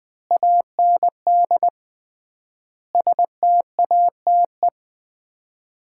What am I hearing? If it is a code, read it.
Morse code, heard as a keyed tone, about 20 wpm, 700 Hz. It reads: AND STATE
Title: 3x Word-Spacing - Sets of 5 Words - 20wpm